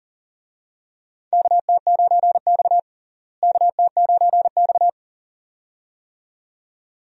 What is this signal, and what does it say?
Morse code, heard as a keyed tone, about 40 wpm, 700 Hz. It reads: KT9X KT9X
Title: USA Call Signs - 4 Characters - Repeated 2x - 40wpm